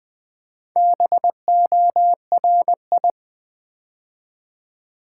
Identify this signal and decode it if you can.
Morse code, heard as a keyed tone, about 20 wpm, 690 Hz. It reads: BORI
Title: Build A Word - 2 Letters At A Time - 6-Letter Words - 20wpm